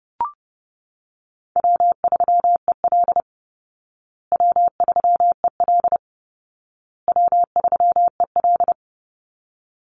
Morse code, heard as a keyed tone, about 30 wpm, 700 Hz. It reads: W3EL W3EL W3EL
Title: Easy Call Signs Repeated 3x - 30wpm